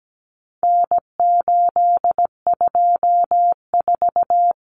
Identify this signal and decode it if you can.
Morse code, heard as a keyed tone, about 17 wpm, 700 Hz. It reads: N824
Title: Common Cut Numbers - 4-Digit 17wpm